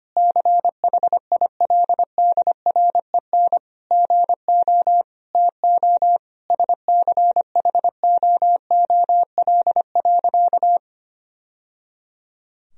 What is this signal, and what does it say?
Morse code, heard as a keyed tone, about 25 wpm, 700 Hz. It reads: CHILDREN GO TO SCHOOL.
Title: Sentences from Top 1000 Words - Review Part 4 - 25wpm